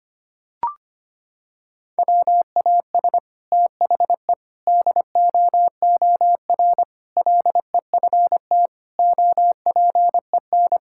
Morse code, heard as a keyed tone, about 25 wpm, 700 Hz. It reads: WAS THE DOOR LEFT OPEN
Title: Sentences from Top 1000 Words - Review Part 6 - 25wpm